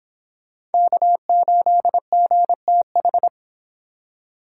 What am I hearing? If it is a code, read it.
Morse code, heard as a keyed tone, about 29 wpm, 700 Hz. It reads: K8GTH